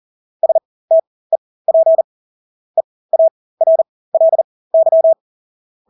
Morse code, heard as a keyed tone, about 40 wpm, spaced as for 20 wpm, 650 Hz. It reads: STEP EARLY
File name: Engelske ord 40-20 wpm